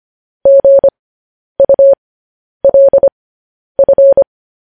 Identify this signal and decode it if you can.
Morse code, heard as a keyed tone, about 25 wpm, spaced as for 10 wpm, 550 Hz. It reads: GULF
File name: podcast / News headlines-1x-25x10